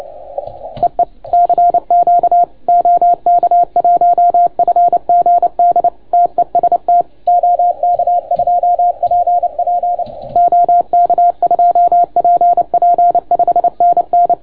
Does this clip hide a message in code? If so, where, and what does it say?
Yes: Morse code from 0.8 to 14.3 s, heard as a keyed tone, about 28 wpm, 690 Hz. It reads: EE CQ OK1FGD TEST OK2PP OK2PP 5NN